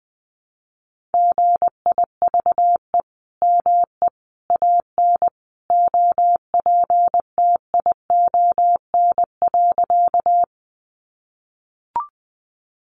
Morse code, heard as a keyed tone, about 19 wpm, 700 Hz. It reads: GIVE ME AN OPTION.
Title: Sentences from Top 1300 Words 20wpm